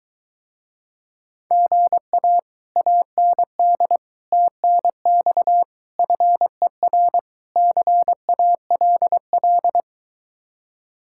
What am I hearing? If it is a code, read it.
Morse code, heard as a keyed tone, about 23 wpm, 700 Hz. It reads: GA AND TNX FER CALL